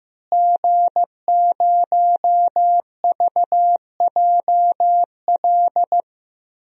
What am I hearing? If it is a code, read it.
Morse code, heard as a keyed tone, about 15 wpm, 700 Hz. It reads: G0VJL
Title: DXCC-100 Call Signs - 5 Characters - Repeated 3x - 15wpm